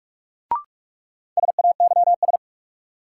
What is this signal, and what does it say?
Morse code, heard as a keyed tone, about 46 wpm, 700 Hz. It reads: SAYS
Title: Sets of 2 Words - New Word Drill - 45wpm